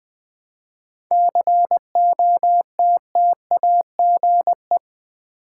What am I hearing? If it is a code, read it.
Morse code, heard as a keyed tone, about 20 wpm, 700 Hz. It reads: COTTAGE
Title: 7 Letter Words - First 3 Letters Given - 20wpm